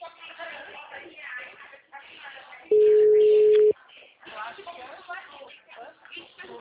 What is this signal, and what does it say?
Ringback tone (Europe, 425 Hz), with speech.
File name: out- -2010- - - .70